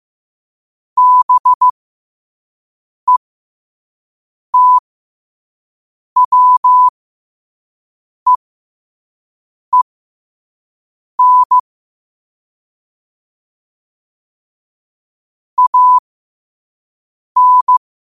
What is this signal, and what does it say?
Morse code, heard as a keyed tone, about 15 wpm, spaced as for 5 wpm, 1000 Hz. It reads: BETWEEN AN
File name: qotdfiles / 5wpm